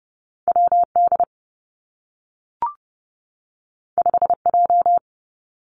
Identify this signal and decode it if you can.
Morse code, heard as a keyed tone, about 30 wpm, 700 Hz. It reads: W D 5 J
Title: Two Challenging Letter-Number Combinations 30wpm